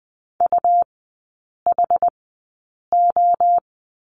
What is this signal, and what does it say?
Morse code, heard as a keyed tone, about 20 wpm, spaced as for 8 wpm, 700 Hz. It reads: UHO
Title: 138 Lesson 23e - Letter f - Three Characters 20wpm